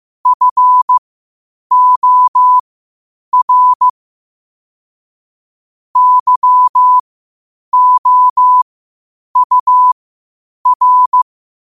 Morse code, heard as a keyed tone, about 15 wpm, spaced as for 8 wpm, 1000 Hz. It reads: FOR YOUR
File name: 8wpm